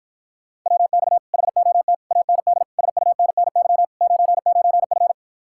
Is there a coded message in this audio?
Yes: Morse code from 0.7 to 5.1 s, heard as a keyed tone, about 54 wpm, 700 Hz. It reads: WX HOT AND SUNNY 99F